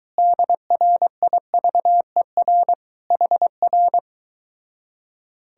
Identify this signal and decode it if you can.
Morse code, heard as a keyed tone, about 23 wpm, 700 Hz. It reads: DRIVER HR